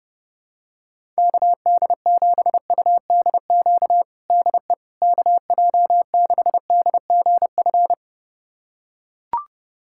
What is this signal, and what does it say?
Morse code, heard as a keyed tone, about 28 wpm, 700 Hz. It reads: KD7UDQ DE KJ6DGF
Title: USA Callsign DE Callsign - 6 Characters - 30wpm